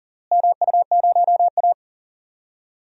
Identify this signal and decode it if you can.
Morse code, heard as a keyed tone, about 40 wpm, 700 Hz. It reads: MU0A
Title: DXCC-100 Call Signs - 4 Characters - Repeated 3x - 40wpm